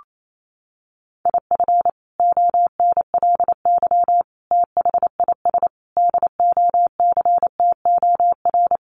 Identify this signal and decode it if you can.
Morse code, heard as a keyed tone, about 28 wpm, 700 Hz. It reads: IF ONLY THIS DOCTOR
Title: Sentences from Top 1500 Words 28wpm